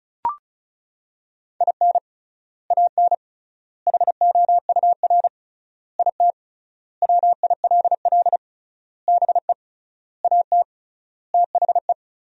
Morse code, heard as a keyed tone, about 36 wpm, 700 Hz. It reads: IN AN HOUR IT WILL BE AT THE
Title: 3x Word-Spacing - Sentences from Top 400 Words - 35wpm